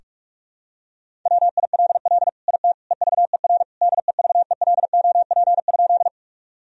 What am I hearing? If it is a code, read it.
Morse code, heard as a keyed tone, about 46 wpm, 700 Hz. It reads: WILL IT EVER DEVELOP?
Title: Sentences from Top 1000 Words - Review Part 2 - 45wpm